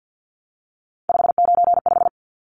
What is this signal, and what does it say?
Morse code, heard as a keyed tone, about 50 wpm, 700 Hz. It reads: 595
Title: Numbers - 3-Digit 50wpm